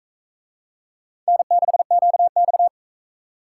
Morse code, heard as a keyed tone, about 43 wpm, 700 Hz. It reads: N6QX